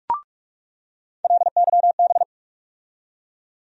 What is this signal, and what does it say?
Morse code, heard as a keyed tone, about 46 wpm, 700 Hz. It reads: LYB
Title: USA Call Sign Postfixes - 3 Letters 45wpm